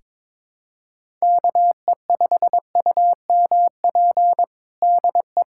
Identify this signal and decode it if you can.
Morse code, heard as a keyed tone, about 21 wpm, 700 Hz. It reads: KE5UMP DE
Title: USA Callsign DE Callsign 22wpm